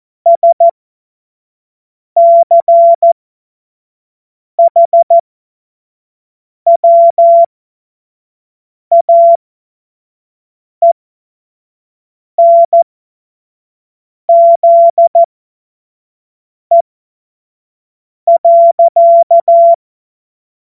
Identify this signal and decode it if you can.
Morse code, heard as a keyed tone, about 14 wpm, spaced as for 5 wpm, 680 Hz. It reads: SCHWAENZE.
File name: Morsekurs / Witze / Witz 05. April 2022 25 F70